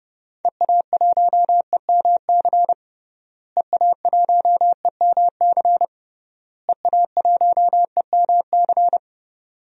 Morse code, heard as a keyed tone, about 32 wpm, 700 Hz. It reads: EA1EMC EA1EMC EA1EMC
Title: DXCC-100 Call Signs - 6 Characters - Repeated 3x - 30wpm